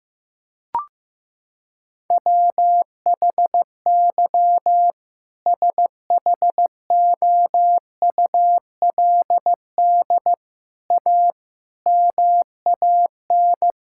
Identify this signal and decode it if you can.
Morse code, heard as a keyed tone, about 15 wpm, 700 Hz. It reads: WHY SHOULD A MAN
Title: Sentences from Top 1600 Words 15wpm